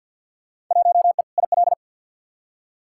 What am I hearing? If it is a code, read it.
Morse code, heard as a keyed tone, about 49 wpm, 700 Hz. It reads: 1E IL
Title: Field Day Exchanges - 50wpm